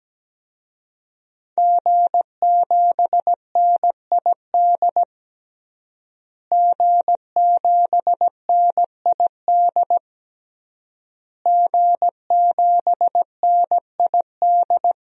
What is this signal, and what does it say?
Morse code, heard as a keyed tone, about 17 wpm, 700 Hz. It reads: G7NID G7NID G7NID
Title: DXCC-100 Call Signs - 5 Characters - Repeated 3x - 17wpm